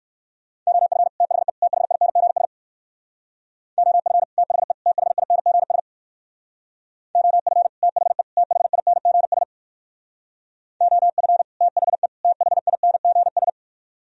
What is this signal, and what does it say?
Morse code, heard as a keyed tone, about 62 wpm, 700 Hz. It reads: OF THE THINGS OF THE THINGS OF THE THINGS OF THE THINGS